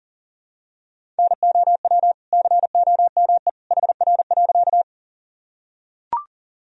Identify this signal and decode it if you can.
Morse code, heard as a keyed tone, about 40 wpm, 700 Hz. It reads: NOW COME HR.
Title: Sentences from Top 200 Words and Abbreviations 40wpm